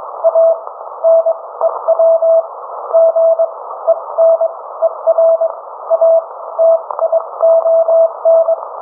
Morse code, heard as a keyed tone, about 22 wpm, spaced as for 17 wpm, 660 Hz. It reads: A NEW GENERATION